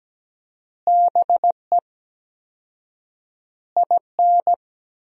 Morse code, heard as a keyed tone, about 17 wpm, 700 Hz. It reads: BE IN
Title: 4x Word-Spacing - Sentences from Top 500 Words - 17wpm